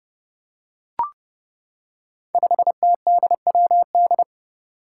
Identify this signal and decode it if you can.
Morse code, heard as a keyed tone, about 32 wpm, 700 Hz. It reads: 5TDWD
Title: Five Challenging Letter-Number Combinations 30wpm